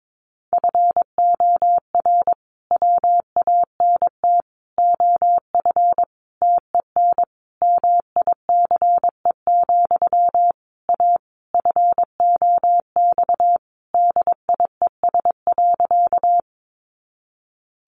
Morse code, heard as a keyed tone, about 22 wpm, 700 Hz. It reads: FOR WANT OF TEN MICE, A FOX DIES.
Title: A Sound Of Thunder 22wpm